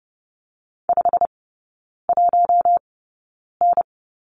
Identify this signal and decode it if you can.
Morse code, heard as a keyed tone, about 30 wpm, spaced as for 9 wpm, 700 Hz. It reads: 51N
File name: course / 30:10wpm / 087 Lesson 15e - Number 5 - Three Characters 30wpm